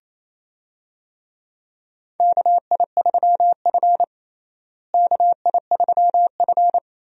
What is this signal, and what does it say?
Morse code, heard as a keyed tone, about 28 wpm, 700 Hz. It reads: KI3F KI3F
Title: USA Call Signs - 4 Characters - Repeated 2x - 28wpm